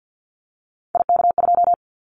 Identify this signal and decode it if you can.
Morse code, heard as a keyed tone, about 50 wpm, 700 Hz. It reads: IK2
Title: Callsign Prefixes for Top 25 DXCC - Repeated 3x - 50wpm